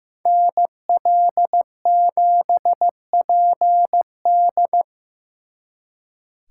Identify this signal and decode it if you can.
Morse code, heard as a keyed tone, about 18 wpm, 700 Hz. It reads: NL7PD
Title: Speed Racing - USA Call Signs 15wpm